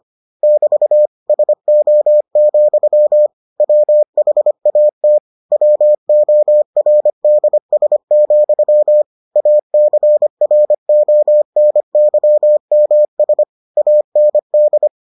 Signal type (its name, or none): Morse code